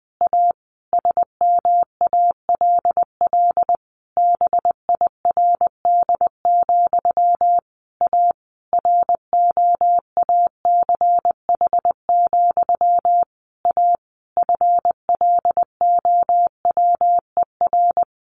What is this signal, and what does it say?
Morse code, heard as a keyed tone, about 20 wpm, 700 Hz. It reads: A SMALL BIRD, A ROACH, A FLOWER
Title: A Sound Of Thunder 20wpm